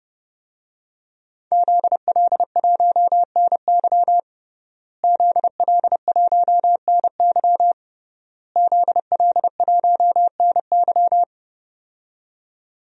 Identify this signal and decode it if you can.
Morse code, heard as a keyed tone, about 30 wpm, 700 Hz. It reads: ZL1NY ZL1NY ZL1NY
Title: DXCC-100 Call Signs - 5 Characters - Repeated 3x - 30wpm